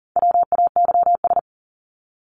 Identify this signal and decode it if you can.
Morse code, heard as a keyed tone, about 40 wpm, 700 Hz. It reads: WAYS
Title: ES:S Suffix - Top 100 Words with Base Words 40wpm